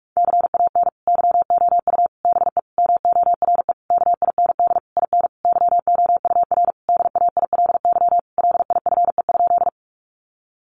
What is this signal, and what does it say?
Morse code, heard as a keyed tone, about 44 wpm, 700 Hz. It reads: CAN YOU BE MORE KIND IN YOUR DAILY LIFE?
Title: Sentences from Top 1100 Words 45wpm